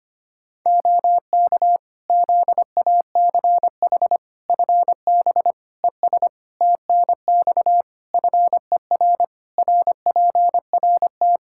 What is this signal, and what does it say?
Morse code, heard as a keyed tone, about 25 wpm, 700 Hz. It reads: OK ZACH FB ES TNX FER RPRT
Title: Practice QSO - Exchange 2 - Info - 25wpm